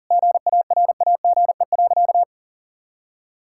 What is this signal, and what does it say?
Morse code, heard as a keyed tone, about 40 wpm, 700 Hz. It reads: GARAGE.
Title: Sentences from Top 1600 Words 40wpm